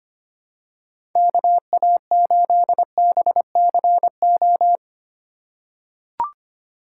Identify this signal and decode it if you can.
Morse code, heard as a keyed tone, about 25 wpm, 700 Hz. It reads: KA8BCO
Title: USA Call Signs - 6 Characters - 2x Character Spacing - 25wpm